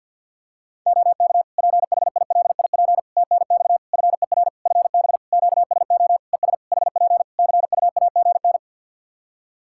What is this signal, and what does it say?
Morse code, heard as a keyed tone, about 54 wpm, 700 Hz. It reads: OK PHILIP TNX FER FB QSO ES HP CUAGN
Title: K1USN SST QSO Practice 50wpm